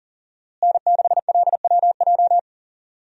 Morse code, heard as a keyed tone, about 40 wpm, 700 Hz. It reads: N6LWJ
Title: USA Call Signs - 5 Characters - Repeated 3x - 40wpm